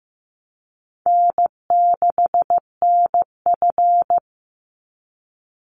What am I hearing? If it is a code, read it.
Morse code, heard as a keyed tone, about 15 wpm, 700 Hz. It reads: N6NF
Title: Easy Call Signs Repeated 3x - 15wpm